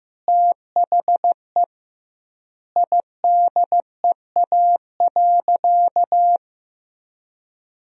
Morse code, heard as a keyed tone, about 15 wpm, 700 Hz. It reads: THE IDEA.
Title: 2x Word-Spacing - Sentences from Top 300 Words - 15wpm